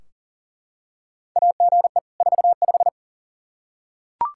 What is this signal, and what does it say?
Morse code, heard as a keyed tone, about 40 wpm, 700 Hz. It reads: AGE 45
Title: Practice QSO - Exchange 3 - More Info - 40wpm